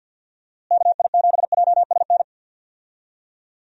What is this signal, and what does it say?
Morse code, heard as a keyed tone, about 46 wpm, 700 Hz. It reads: KI7JSN
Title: USA Call Signs - 6 Characters - Repeated 3x - 50wpm